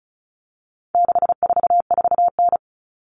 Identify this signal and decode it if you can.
Morse code, heard as a keyed tone, about 34 wpm, 700 Hz. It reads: 644N
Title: All Cut Numbers - 4-Digit 35wpm